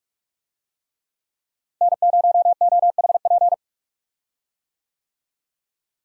Morse code, heard as a keyed tone, about 41 wpm, 700 Hz. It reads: N0OHP
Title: USA Call Signs - 5 Characters - Repeated 3x - 45wpm